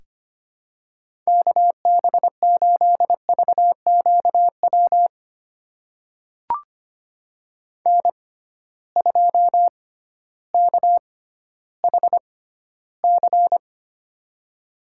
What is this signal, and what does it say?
Morse code, heard as a keyed tone, about 25 wpm, spaced as for 13 wpm, 700 Hz. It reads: KB8VQW N2KHC
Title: USA Call Signs - 6x Character Spacing - 25wpm